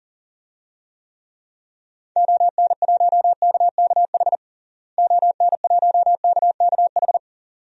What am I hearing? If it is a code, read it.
Morse code, heard as a keyed tone, about 40 wpm, 700 Hz. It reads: ON1KKH ON1KKH
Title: DXCC-100 Call Signs - 6 Characters - Repeated 2x - 40wpm